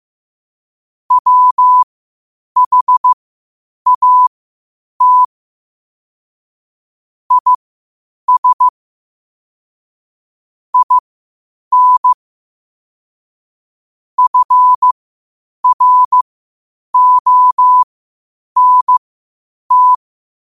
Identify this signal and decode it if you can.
Morse code, heard as a keyed tone, about 15 wpm, spaced as for 8 wpm, 1000 Hz. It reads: WHAT IS IN FRONT
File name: qotdfiles / 8wpm